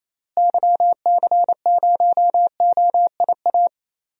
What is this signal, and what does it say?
Morse code, heard as a keyed tone, about 28 wpm, 700 Hz. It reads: YC0OIA
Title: DXCC-100 Call Signs - 6 Characters - Repeated 2x - 28wpm